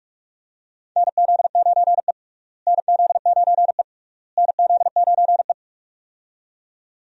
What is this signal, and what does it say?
Morse code, heard as a keyed tone, about 46 wpm, 700 Hz. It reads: NZ9E NZ9E NZ9E
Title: USA Call Signs - 4 Characters - Repeated 3x - 45wpm